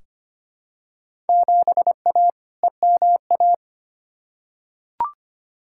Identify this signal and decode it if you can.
Morse code, heard as a keyed tone, about 25 wpm, 700 Hz. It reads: 7A EMA
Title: Field Day Exchanges Encore - 25wpm